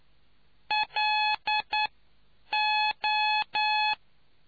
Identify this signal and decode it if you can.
Morse code, heard as a keyed tone, about 9 wpm, 850 Hz. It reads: L O